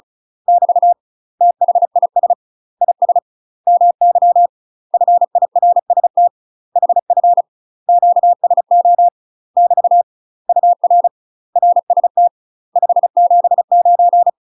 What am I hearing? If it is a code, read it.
Morse code, heard as a keyed tone, about 34 wpm, 700 Hz. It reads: <BT> THIS IS MY FIRST HF QSO <BT> UR RST 579
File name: UK MASTER QSOs TRAINING / 35WPM / UK MASTER QSOs 0003